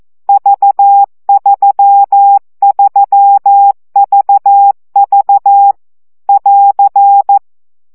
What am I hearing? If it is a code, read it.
Morse code, heard as a keyed tone, about 14 wpm, 800 Hz. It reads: V33VV <AR>
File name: CW gyakorlas v 3 12 baud